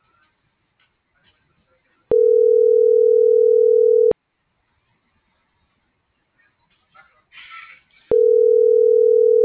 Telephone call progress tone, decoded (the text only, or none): ringback tone